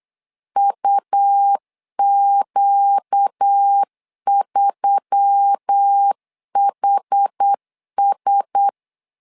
Morse code, heard as a keyed tone, about 8 wpm, 800 Hz. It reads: UQ3HS
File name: LEZIONI / 5 LBVC / 5 L 8wpm 2831